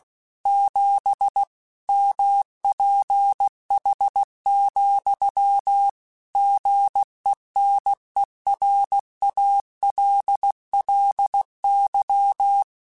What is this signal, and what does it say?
Morse code, heard as a keyed tone, about 16 wpm, 800 Hz. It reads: 7 MPH, GENERALLY
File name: Weather Sets / Weather Sets-mp3s / Weather 4 24 @15